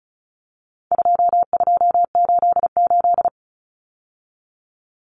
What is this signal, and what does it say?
Morse code, heard as a keyed tone, about 34 wpm, 700 Hz. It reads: 2288
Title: Common Cut Numbers - 4-Digit 35wpm